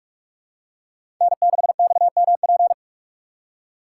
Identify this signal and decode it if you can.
Morse code, heard as a keyed tone, about 42 wpm, 700 Hz. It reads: N6XMP
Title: USA Call Signs - 5 Characters - 6x Character Spacing - 45wpm